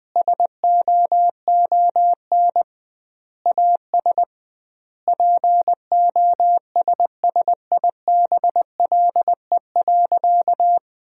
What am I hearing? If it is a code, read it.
Morse code, heard as a keyed tone, about 20 wpm, 700 Hz. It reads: SOON AS POSSIBLE.
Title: 2x Word-Spacing - Sentences from Top 500 Words - 20wpm